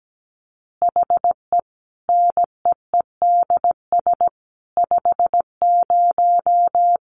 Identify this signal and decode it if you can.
Morse code, heard as a keyed tone, about 17 wpm, 700 Hz. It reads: HE NEEDS 50
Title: Additional Easy Phrases 17wpm